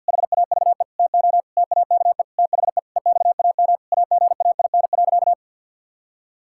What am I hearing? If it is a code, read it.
Morse code, heard as a keyed tone, about 50 wpm, 700 Hz. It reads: HAVE TO TAKE THE EXAM AGAIN.
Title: Sentences from Top 1000 Words - Review Part 5 - 50wpm